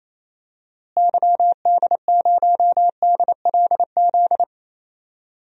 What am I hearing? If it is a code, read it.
Morse code, heard as a keyed tone, about 28 wpm, 700 Hz. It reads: YD0DLZ
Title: DXCC-100 Call Signs - 6 Characters - Repeated 3x - 28wpm